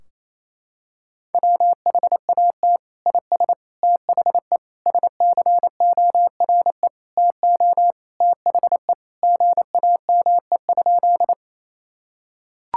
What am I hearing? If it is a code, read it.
Morse code, heard as a keyed tone, about 27 wpm, 700 Hz. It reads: WHAT IS THE SCORE TO THE GAME?
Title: Sentences from Top 1600 Words 28wpm